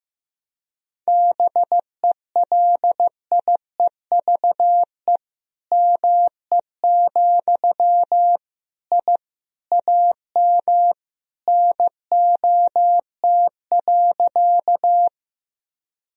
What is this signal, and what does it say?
Morse code, heard as a keyed tone, about 15 wpm, 700 Hz. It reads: BELIEVE ME, I AM NOT.
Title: Sentences from Top 1000 Words - Review Part 2 - 15wpm